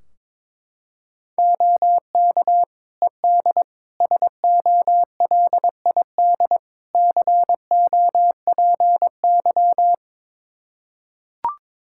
Morse code, heard as a keyed tone, about 21 wpm, 700 Hz. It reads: OK ED SOLID COPY
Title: Practice QSO - Exchange 3 - More Info - 22wpm